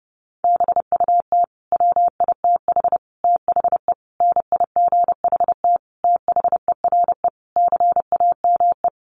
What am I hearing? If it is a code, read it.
Morse code, heard as a keyed tone, about 28 wpm, 700 Hz. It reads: BUT WITH THE NIGHT THERE CAME
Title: Sentences from Top 1500 Words 30wpm